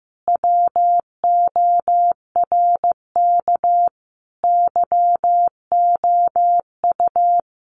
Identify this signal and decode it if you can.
Morse code, heard as a keyed tone, about 15 wpm, 700 Hz. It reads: WORK YOU
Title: Common QSO Phrases 15wpm